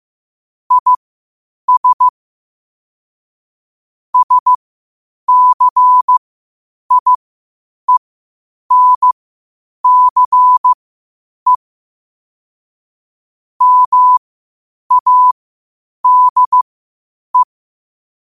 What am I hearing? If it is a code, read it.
Morse code, heard as a keyed tone, about 15 wpm, spaced as for 8 wpm, 1000 Hz. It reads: IS SCIENCE MADE
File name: qotdfiles / 8wpm